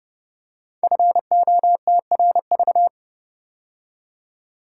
Morse code, heard as a keyed tone, about 30 wpm, 700 Hz. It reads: FOTRV
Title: Code Groups - 5 Letters - 30wpm